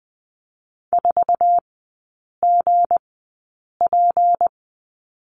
Morse code, heard as a keyed tone, about 20 wpm, spaced as for 8 wpm, 690 Hz. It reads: 4GP
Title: 159 Lesson 26e - Letter g - Three Characters 20wpm